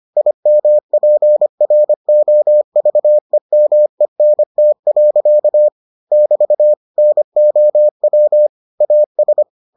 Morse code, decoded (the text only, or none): IMPROVEMENT. <BT> NOW AS